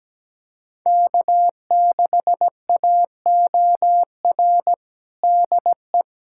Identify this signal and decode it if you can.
Morse code, heard as a keyed tone, about 17 wpm, 700 Hz. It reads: K6AOR DE
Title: USA Callsign DE Callsign 17wpm